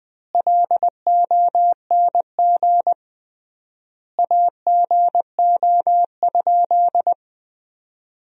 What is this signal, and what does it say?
Morse code, heard as a keyed tone, about 20 wpm, 700 Hz. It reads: LONG AGO?
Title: 3x Word-Spacing - Sentences from Top 500 Words - 20wpm